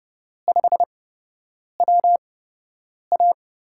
Morse code, heard as a keyed tone, about 30 wpm, spaced as for 8 wpm, 700 Hz. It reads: 5WA
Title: USA Call Signs - 5 Characters - 8x Character Spacing - 30wpm